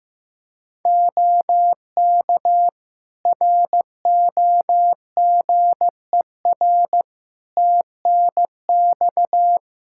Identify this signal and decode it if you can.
Morse code, heard as a keyed tone, about 15 wpm, 700 Hz. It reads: OK ROGER TNX
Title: Practice QSO - Exchange 4 - Ending - 15wpm